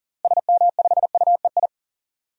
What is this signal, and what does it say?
Morse code, heard as a keyed tone, about 40 wpm, 700 Hz. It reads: SM5UEI
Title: DXCC-100 Call Signs - 6 Characters - Repeated 2x - 40wpm